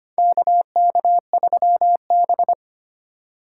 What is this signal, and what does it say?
Morse code, heard as a keyed tone, about 25 wpm, 700 Hz. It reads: KK3B
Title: USA Call Signs - 4 Characters - 2x Character Spacing - 25wpm